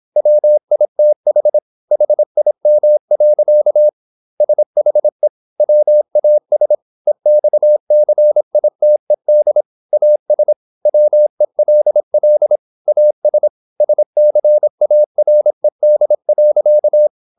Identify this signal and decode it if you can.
Morse code, heard as a keyed tone, about 26 wpm, 600 Hz. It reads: WITH HIM. SHE WAS EXCITED AS WELL AS SCARED.
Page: Never 26wpm | CW med Gnister